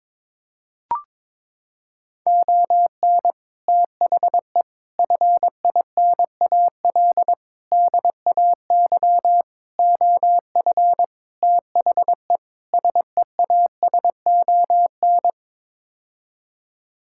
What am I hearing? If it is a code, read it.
Morse code, heard as a keyed tone, about 22 wpm, 700 Hz. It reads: ON THE FINAL DAY OF THE SEASON
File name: practice / 22wpm / Sets of 7 Words - 22wpm